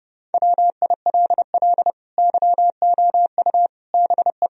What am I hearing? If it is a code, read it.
Morse code, heard as a keyed tone, about 32 wpm, 700 Hz. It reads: WILL YOU BE
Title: Sentences from Top 1000 Words - Review Part 2 - 30wpm